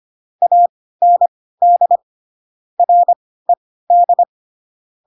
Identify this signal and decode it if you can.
Morse code, heard as a keyed tone, about 25 wpm, spaced as for 16 wpm, 700 Hz. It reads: AND RED
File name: Pinocchio / Pinocchio ch1 10 25 ik1hge